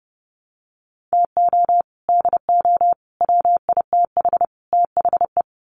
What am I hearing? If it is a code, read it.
Morse code, heard as a keyed tone, about 30 wpm, 700 Hz. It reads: TO DO WITH THE
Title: Sets of 4 Words 30wpm